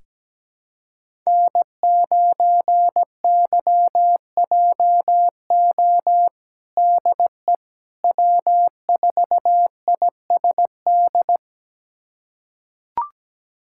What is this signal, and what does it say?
Morse code, heard as a keyed tone, about 17 wpm, 700 Hz. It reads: N9YJO DE W4ISD
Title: USA Callsign DE Callsign - 5 Characters - 17wpm